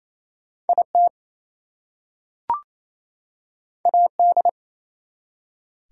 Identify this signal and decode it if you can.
Morse code, heard as a keyed tone, about 27 wpm, 700 Hz. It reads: IT AD
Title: Two Letter Words - 28wpm